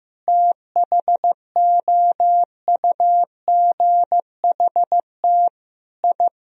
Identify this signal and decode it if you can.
Morse code, heard as a keyed tone, about 15 wpm, 700 Hz. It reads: THOUGHT I
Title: Sentences from Top 1000 Words - Review Part 4 - 15wpm